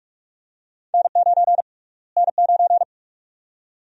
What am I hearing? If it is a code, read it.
Morse code, heard as a keyed tone, about 46 wpm, 700 Hz. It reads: N9 N9
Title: USA Call Sign Prefixes - Repeated 2x - 45wpm